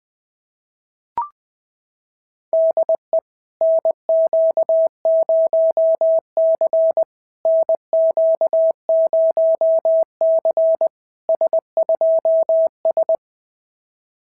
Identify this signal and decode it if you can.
Morse code, heard as a keyed tone, about 20 wpm, 660 Hz. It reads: DE NQ0C NQ0C S2S
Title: SOTA QSO Practice 20wpm